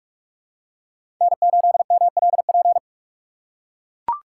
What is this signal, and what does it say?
Morse code, heard as a keyed tone, about 44 wpm, 700 Hz. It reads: N8MLP
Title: USA Call Signs - 5 Characters - 6x Character Spacing - 45wpm